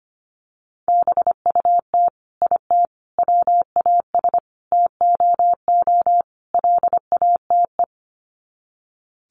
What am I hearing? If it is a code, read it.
Morse code, heard as a keyed tone, about 25 wpm, 700 Hz. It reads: BUT IT WAS TOO LATE
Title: Sets of 5 Words Encore2 25wpm